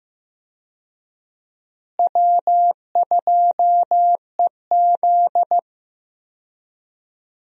Morse code, heard as a keyed tone, about 15 wpm, 700 Hz. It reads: W2EZ
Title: USA Call Signs - 4 Characters - Repeated 3x - 15wpm